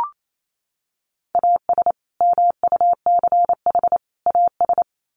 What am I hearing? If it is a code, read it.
Morse code, heard as a keyed tone, about 28 wpm, 700 Hz. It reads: AS MUCH AS
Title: Sets of 3 Words 28wpm